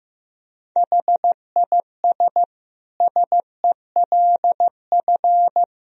Morse code, heard as a keyed tone, about 15 wpm, 700 Hz. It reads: HIS SELF
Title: Sentences from Top 1000 Words - Review Part 1 - 15wpm